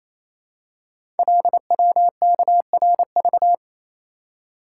Morse code, heard as a keyed tone, about 29 wpm, 700 Hz. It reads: LWKRV
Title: Code Groups - 5 Letters - 28wpm